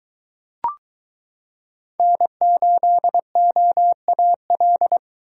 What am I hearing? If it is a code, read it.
Morse code, heard as a keyed tone, about 23 wpm, 700 Hz. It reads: N8OAL